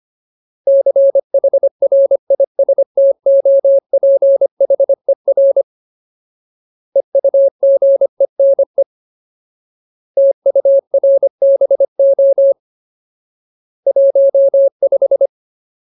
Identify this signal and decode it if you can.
Morse code, heard as a keyed tone, about 25 wpm, 550 Hz. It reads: CHRISTOPHER EUGENE TURBO 15
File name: Ultimate Word List / 25WPM Ultimate